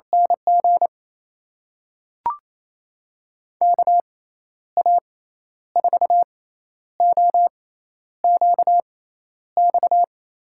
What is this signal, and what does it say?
Morse code, heard as a keyed tone, about 28 wpm, spaced as for 11 wpm, 700 Hz. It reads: NG KA4OQX
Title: USA Call Signs - 6 Characters - 6x Character Spacing - 28wpm